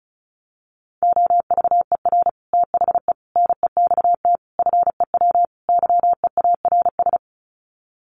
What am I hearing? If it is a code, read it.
Morse code, heard as a keyed tone, about 34 wpm, 700 Hz. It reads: OVER THE NEXT FEW YEARS
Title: Sets of 5 Words Encore 35wpm